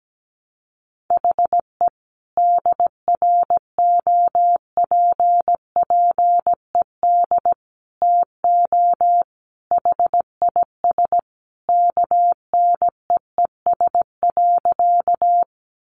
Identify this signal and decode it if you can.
Morse code, heard as a keyed tone, about 17 wpm, 700 Hz. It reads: HE DROPPED TO HIS KNEES.
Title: A Sound Of Thunder 17wpm